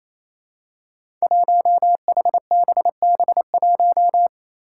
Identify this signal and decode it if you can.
Morse code, heard as a keyed tone, about 27 wpm, 700 Hz. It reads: 1HBB1
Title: Five Challenging Letter-Number Combinations 28wpm